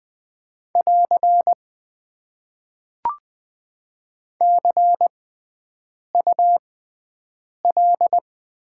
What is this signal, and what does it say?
Morse code, heard as a keyed tone, about 20 wpm, spaced as for 7 wpm, 700 Hz. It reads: <AR> CUL
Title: Top 90 QSO Elements - 6x Character Spacing - 20wpm